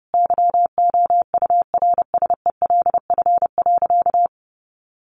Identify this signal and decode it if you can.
Morse code, heard as a keyed tone, about 30 wpm, 700 Hz. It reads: YOURSELF.
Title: Sentences from Top 600 Words 30wpm